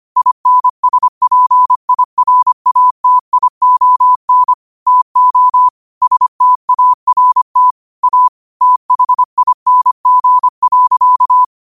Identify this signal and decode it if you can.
Morse code, heard as a keyed tone, about 25 wpm, 1000 Hz. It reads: INSPIRATION TO START A THING.
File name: qotdfiles / 25wpm